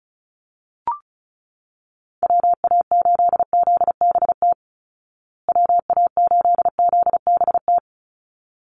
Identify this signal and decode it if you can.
Morse code, heard as a keyed tone, about 34 wpm, 700 Hz. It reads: WA8ZBT WA8ZBT
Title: Call Signs Repeated 3x - 35wpm